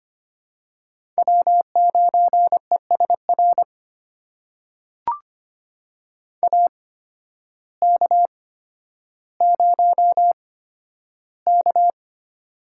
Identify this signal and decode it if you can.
Morse code, heard as a keyed tone, about 25 wpm, spaced as for 12 wpm, 700 Hz. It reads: W9ESR AK0K
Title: USA Call Signs - 8x Character Spacing - 25wpm